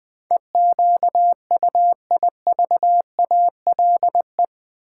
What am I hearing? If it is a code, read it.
Morse code, heard as a keyed tone, about 20 wpm, 700 Hz. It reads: EQUIVALE
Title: Build A Word - 4 Letters At A Time - 10-Letter Words - 20wpm